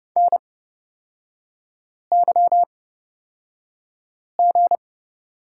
Morse code, heard as a keyed tone, about 30 wpm, spaced as for 5 wpm, 700 Hz. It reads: NYG
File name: Lesson 37e - Letter x - Three Characters 30wpm